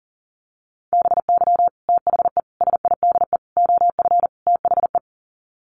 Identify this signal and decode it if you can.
Morse code, heard as a keyed tone, about 40 wpm, 700 Hz. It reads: BY THE SIDE OF THE
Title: Sets of 5 Words Encore2 40wpm